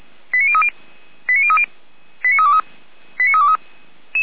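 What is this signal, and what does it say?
Five-tone selective call (ZVEI), heard as quick runs of five tones, tones of about 70 ms. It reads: 89030 89030 89323 89323
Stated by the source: ZVEI five-tone call 89058 received by SDR, archived on server of Feuerwehr Orlamünde (audio.f85360.feuerwehr-orlamuende.org).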